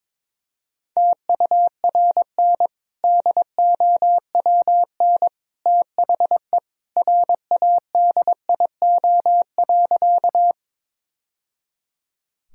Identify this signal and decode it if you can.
Morse code, heard as a keyed tone, about 22 wpm, 700 Hz. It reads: TURN DOWN THE RADIO.
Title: Sentences from Top 1000 Words - Review Part 3 - 22wpm